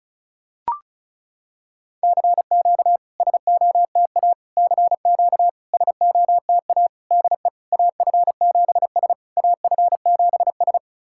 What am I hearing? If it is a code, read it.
Morse code, heard as a keyed tone, about 36 wpm, 700 Hz. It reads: CQ SOTA CQ SOTA DE AF7S AF7S
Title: SOTA QSO Practice 35wpm